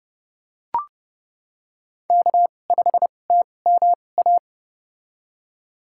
Morse code, heard as a keyed tone, about 30 wpm, spaced as for 22 wpm, 700 Hz. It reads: K5TMA
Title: USA Call Signs - 5 Characters - 2x Character Spacing - 30wpm